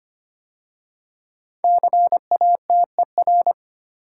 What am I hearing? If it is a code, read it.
Morse code, heard as a keyed tone, about 25 wpm, 700 Hz. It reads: CATER
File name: 5 Letter Words - First 3 Letters Given - 25wpm